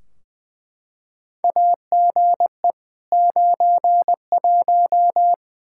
Morse code, heard as a keyed tone, about 20 wpm, 700 Hz. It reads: AGE 91
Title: Practice QSO - Exchange 3 - More Info - 20wpm